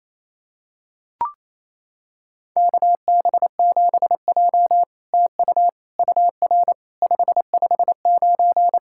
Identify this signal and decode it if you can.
Morse code, heard as a keyed tone, about 29 wpm, 690 Hz. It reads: KB7J TU UR 559
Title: SOTA QSO Practice 28wpm